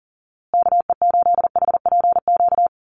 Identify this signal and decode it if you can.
Morse code, heard as a keyed tone, about 40 wpm, 700 Hz. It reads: KE8HPQ
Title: USA Call Signs Repeated 3x 40wpm